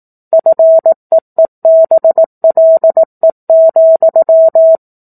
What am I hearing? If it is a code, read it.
Morse code, heard as a keyed tone, about 18 wpm, 650 Hz. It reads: FEEBLE,
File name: 04-Chapter-04 18 WPM